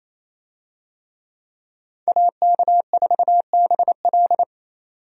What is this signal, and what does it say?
Morse code, heard as a keyed tone, about 28 wpm, 700 Hz. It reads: AK4BL
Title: USA Call Signs - 5 Characters - 28wpm